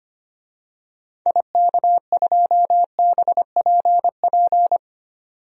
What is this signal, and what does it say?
Morse code, heard as a keyed tone, about 25 wpm, 700 Hz. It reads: IK2BPP
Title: DXCC-100 Call Signs - 6 Characters - 25wpm